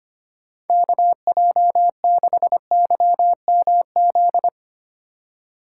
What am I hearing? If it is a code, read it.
Morse code, heard as a keyed tone, about 25 wpm, 700 Hz. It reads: KJ6YMZ
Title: USA Call Signs - 6 Characters - 3x Character Spacing - 25wpm